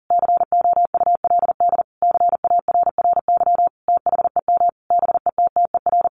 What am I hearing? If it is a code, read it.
Morse code, heard as a keyed tone, about 40 wpm, 700 Hz. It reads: COULD CARRY THEM BETTER
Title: Sentences from Top 1200 Words 40wpm